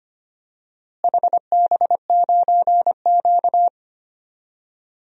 Morse code, heard as a keyed tone, about 25 wpm, 700 Hz. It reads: HB9Q
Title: DXCC-100 Call Signs - 4 Characters - Repeated 2x - 25wpm